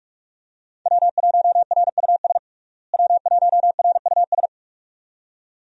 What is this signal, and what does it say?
Morse code, heard as a keyed tone, about 42 wpm, 700 Hz. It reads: W1RUS W1RUS
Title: USA Call Signs - 5 Characters - Repeated 2x - 45wpm